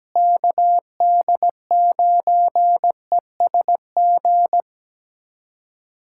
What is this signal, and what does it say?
Morse code, heard as a keyed tone, about 17 wpm, 700 Hz. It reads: KD9ESG
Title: USA Callsign DE Callsign 17wpm